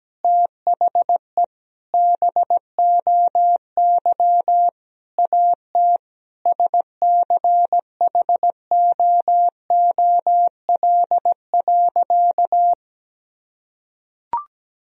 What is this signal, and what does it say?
Morse code, heard as a keyed tone, about 17 wpm, 700 Hz. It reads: THE BOY AT SCHOOL.
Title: Sentences from Top 300 Words and Abbreviations 17wpm